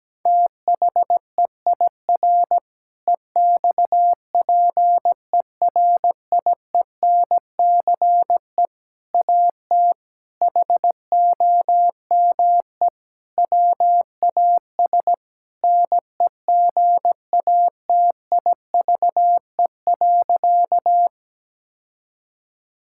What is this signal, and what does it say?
Morse code, heard as a keyed tone, about 17 wpm, 700 Hz. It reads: THEIR EXPERIENCE AT HOME WAS NEGATIVE.
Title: Sentences from Top 1000 Words - Review Part 6 - 17wpm